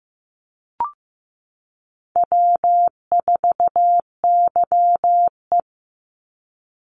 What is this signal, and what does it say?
Morse code, heard as a keyed tone, about 15 wpm, 700 Hz. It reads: W4YE
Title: Easy Call Signs Repeated 2x - 15wpm